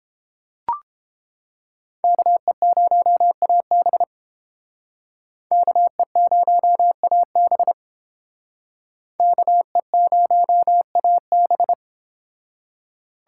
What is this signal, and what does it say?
Morse code, heard as a keyed tone, about 30 wpm, 700 Hz. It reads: KE0AB KE0AB KE0AB